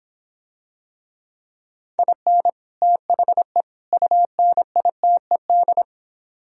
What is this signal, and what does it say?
Morse code, heard as a keyed tone, about 25 wpm, 700 Hz. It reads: IN THE UNITED